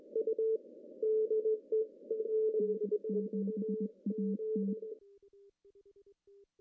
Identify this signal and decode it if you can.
Morse code, heard as a keyed tone, about 20 wpm, 200 Hz. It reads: N7AT
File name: CW / cqww-cw-k5tr